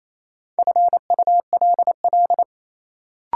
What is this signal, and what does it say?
Morse code, heard as a keyed tone, about 28 wpm, 700 Hz. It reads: FULL
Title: Sentences from Top 500 Words - New Word Drill - 28wpm